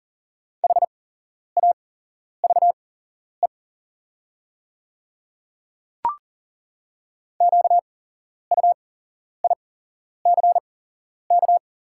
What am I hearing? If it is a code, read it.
Morse code, heard as a keyed tone, about 40 wpm, spaced as for 11 wpm, 690 Hz. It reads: HAVE QUICK
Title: 8x Character Spacing - Top 500 Words - 40wpm